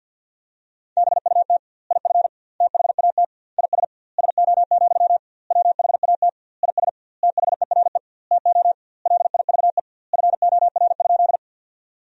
Morse code, heard as a keyed tone, about 51 wpm, 700 Hz. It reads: BUT IF THAT IS SO, WHAT IS THERE TO LIVE FOR?
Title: 2x Word-Spacing - Sentences from Top 200 Words - 50wpm